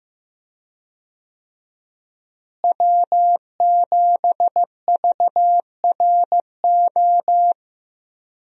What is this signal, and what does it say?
Morse code, heard as a keyed tone, about 15 wpm, 700 Hz. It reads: W7VRO
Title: USA Call Signs - 5 Characters - Repeated 3x - 15wpm